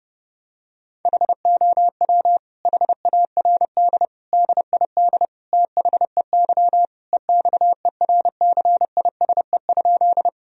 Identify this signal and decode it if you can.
Morse code, heard as a keyed tone, about 32 wpm, 700 Hz. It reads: HOW HARD DID THEY EXERCISE?
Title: Sentences from Top 1000 Words - Review Part 4 - 30wpm